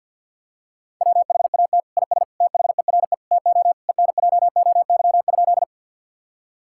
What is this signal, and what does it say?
Morse code, heard as a keyed tone, about 50 wpm, 700 Hz. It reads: WHAT IS THERE TO ENJOY?
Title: Sentences from Top 1000 Words - Review Part 5 - 50wpm